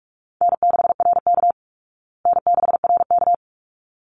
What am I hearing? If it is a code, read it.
Morse code, heard as a keyed tone, about 44 wpm, 700 Hz. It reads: N6RK N6RK
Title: Easy Call Signs Repeated 2x - 45wpm